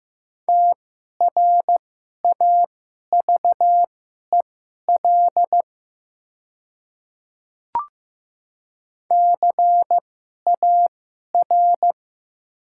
Morse code, heard as a keyed tone, about 15 wpm, spaced as for 11 wpm, 700 Hz. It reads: TRAVEL CAR
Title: 2x Character Spacing - Top 500 Words - 15wpm